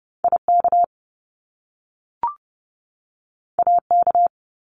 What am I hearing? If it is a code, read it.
Morse code, heard as a keyed tone, about 30 wpm, 700 Hz. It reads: IK AK
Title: Top Two Letter Combinations in Words 30wpm